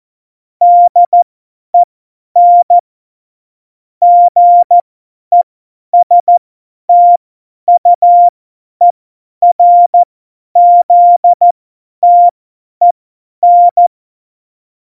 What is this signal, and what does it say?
Morse code, heard as a keyed tone, about 14 wpm, spaced as for 10 wpm, 700 Hz. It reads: DEN GESTUERZTEN
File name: Morsekurs / Witze / Witz 13. Februar 2018, 50 F70